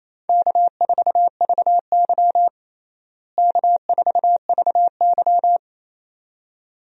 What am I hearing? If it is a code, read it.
Morse code, heard as a keyed tone, about 28 wpm, 700 Hz. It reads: K4VY K4VY
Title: USA Call Signs - 4 Characters - Repeated 2x - 28wpm